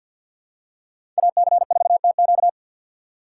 Morse code, heard as a keyed tone, about 53 wpm, 700 Hz. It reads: AC4TQ
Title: USA Call Signs - 5 Characters - 4x Character Spacing - 50wpm